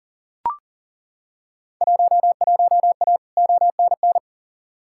Morse code, heard as a keyed tone, about 38 wpm, 700 Hz. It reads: 11A ONN
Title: Field Day Exchanges Encore - 40wpm